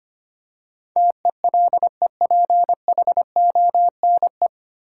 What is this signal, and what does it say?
Morse code, heard as a keyed tone, about 25 wpm, 700 Hz. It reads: TELEPHONE
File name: practice / 25wpm / 9 Letter Words - First 3 Letters Given - 25wpm